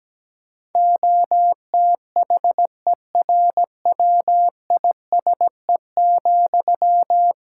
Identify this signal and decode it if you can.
Morse code, heard as a keyed tone, about 17 wpm, 700 Hz. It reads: OTHERWISE,
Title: Sentences from Top 1600 Words 17wpm